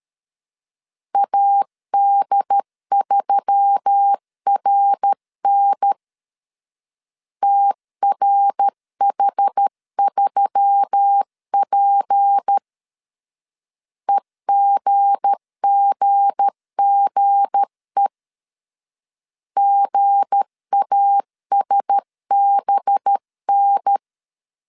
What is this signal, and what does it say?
Morse code, heard as a keyed tone, about 13 wpm, 800 Hz. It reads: AD3RN TRH3P EGGGE GASBN
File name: LEZIONI / 7 GF9Z / 7 G 12wpm 2055